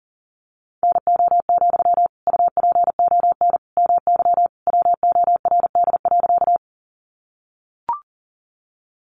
Morse code, heard as a keyed tone, about 40 wpm, 700 Hz. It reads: NO, UPON MY WORD.
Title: Sentences from Top 1100 Words 40wpm